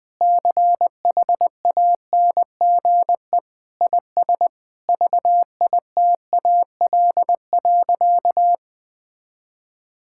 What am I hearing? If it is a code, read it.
Morse code, heard as a keyed tone, about 20 wpm, 700 Hz. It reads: CHANGE IS VITAL.
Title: Sentences with 8-Word Phrases - 20wpm